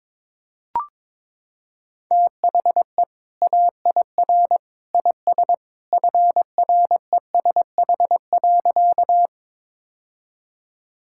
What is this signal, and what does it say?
Morse code, heard as a keyed tone, about 22 wpm, 700 Hz. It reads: THE AIR IS FRESH.
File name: practice / 22wpm / Sentences with 2-Word Phrases - 22wpm